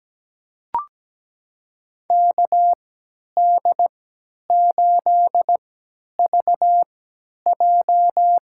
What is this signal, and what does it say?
Morse code, heard as a keyed tone, about 17 wpm, spaced as for 10 wpm, 700 Hz. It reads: KD8VJ
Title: USA Call Signs - 3x Character Spacing - 17wpm